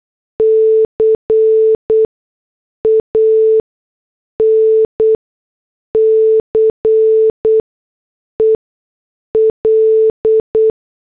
Morse code, heard as a keyed tone, about 8 wpm, spaced as for 6 wpm, 440 Hz. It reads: CANCEL
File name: morse / headlines